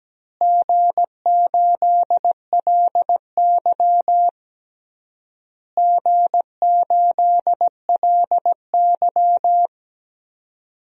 Morse code, heard as a keyed tone, about 17 wpm, 700 Hz. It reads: G8LY G8LY
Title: DXCC-100 Call Signs - 4 Characters - Repeated 3x - 17wpm